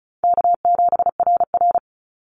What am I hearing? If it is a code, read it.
Morse code, heard as a keyed tone, about 34 wpm, 700 Hz. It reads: K7RR
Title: Easy Call Signs Repeated 3x - 35wpm